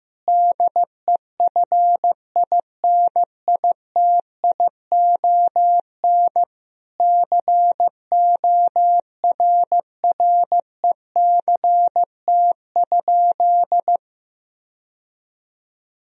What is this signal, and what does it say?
Morse code, heard as a keyed tone, about 16 wpm, 700 Hz. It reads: DEFINITION CORRECT?
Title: Sentences from Top 1000 Words - Review Part 4 - 15wpm